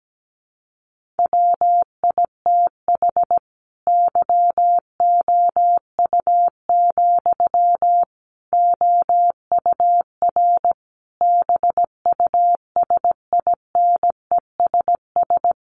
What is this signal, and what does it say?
Morse code, heard as a keyed tone, about 17 wpm, 700 Hz. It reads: WITH YOU, OUR BUSINESS
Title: Sentences from Top 1400 Words 17wpm